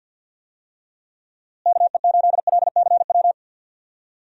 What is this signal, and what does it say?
Morse code, heard as a keyed tone, about 52 wpm, 700 Hz. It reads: KE8LCW
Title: USA Call Signs - 6 Characters - Repeated 2x - 50wpm